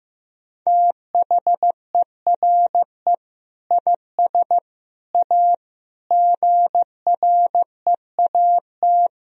Morse code, heard as a keyed tone, about 15 wpm, 700 Hz. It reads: THERE IS A GREAT
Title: Sentences from Top 1000 Words - Review Part 4 - 15wpm